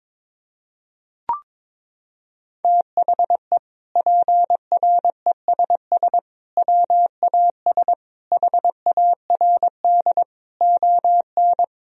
Morse code, heard as a keyed tone, about 22 wpm, 700 Hz. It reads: THE PRESS WAS HARD ON
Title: Sentences from Top 1000 Words - Review Part 1 - 22wpm